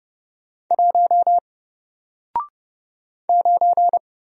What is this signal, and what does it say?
Morse code, heard as a keyed tone, about 30 wpm, 700 Hz. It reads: 1 9
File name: Single-Letter-Number-30wpm